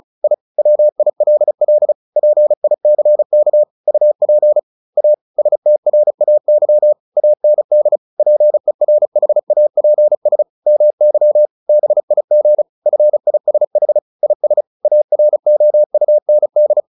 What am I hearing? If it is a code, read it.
Morse code, heard as a keyed tone, about 35 wpm, 600 Hz. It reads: I WILL PICK UP A STRAY AND PERHAPS MY BIG FISH IS AROUND
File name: the old man and the sea / the old man and the sea 1 35wpm